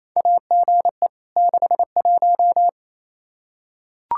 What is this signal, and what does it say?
Morse code, heard as a keyed tone, about 28 wpm, 700 Hz. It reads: AGE 61
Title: K1USN SST QSO Practice 28wpm